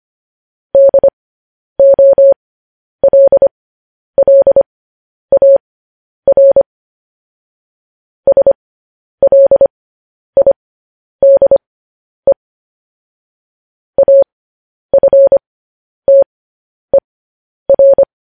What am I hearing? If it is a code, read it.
Morse code, heard as a keyed tone, about 25 wpm, spaced as for 10 wpm, 550 Hz. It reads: DOLLAR SLIDE AFTER
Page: News Headlines in Morse Code at 10 WPM